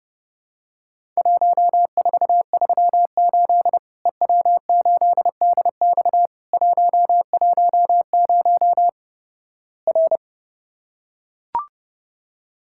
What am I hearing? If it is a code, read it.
Morse code, heard as a keyed tone, about 30 wpm, 700 Hz. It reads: 1438 EW8DX 110 R
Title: Worked All Europe DX Contest - QTC Exchange - 30wpm